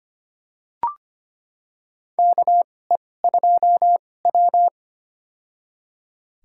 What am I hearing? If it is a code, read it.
Morse code, heard as a keyed tone, about 25 wpm, spaced as for 18 wpm, 700 Hz. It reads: KE2W
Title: USA Call Signs - 4 Characters - 2x Character Spacing - 25wpm